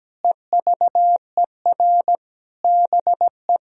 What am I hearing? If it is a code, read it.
Morse code, heard as a keyed tone, about 17 wpm, 700 Hz. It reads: EVER BE
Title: Sentences from Top 1000 Words - Review Part 4 - 17wpm